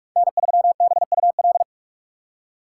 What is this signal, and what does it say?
Morse code, heard as a keyed tone, about 46 wpm, 700 Hz. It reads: N3BUL
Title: USA Call Signs - 5 Characters - 2x Character Spacing - 45wpm